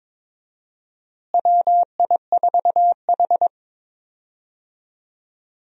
Morse code, heard as a keyed tone, about 22 wpm, 700 Hz. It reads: WI4H
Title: USA Call Signs - 4 Characters - Repeated 2x - 22wpm